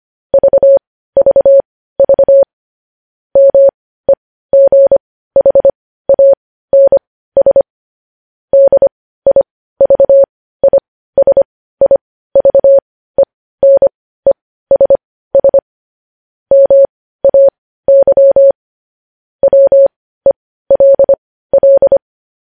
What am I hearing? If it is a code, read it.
Morse code, heard as a keyed tone, about 25 wpm, spaced as for 15 wpm, 550 Hz. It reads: VVV MEGHANS DIVISIVENESS MAY WELL
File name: podcast / News headlines-1x-25x15